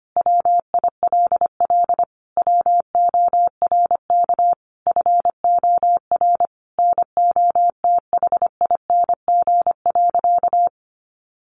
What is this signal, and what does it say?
Morse code, heard as a keyed tone, about 25 wpm, 700 Hz. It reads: WILL WORK FOR NOTHING.
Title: Sentences from Top 1400 Words 25wpm